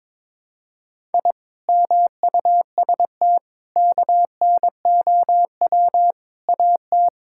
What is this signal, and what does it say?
Morse code, heard as a keyed tone, about 22 wpm, 700 Hz. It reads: I MUST KNOW AT
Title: Sentences from Top 1000 Words - Review Part 5 - 22wpm